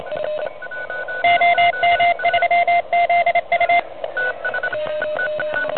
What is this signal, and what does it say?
Morse code, heard as a keyed tone, about 27 wpm, 2100 Hz, with noise. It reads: OM3ZU
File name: truSDX / NZ contest